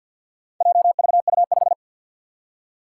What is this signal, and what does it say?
Morse code, heard as a keyed tone, about 49 wpm, 700 Hz. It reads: JVU5
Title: Four Challenging Letter-Number Combinations 50wpm